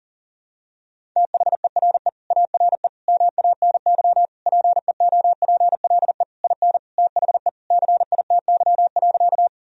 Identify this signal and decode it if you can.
Morse code, heard as a keyed tone, about 40 wpm, 700 Hz. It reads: THERE ARE MANY PEOPLE IN THE CITY.
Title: Sentences from Top 1000 Words - Review Part 1 - 40wpm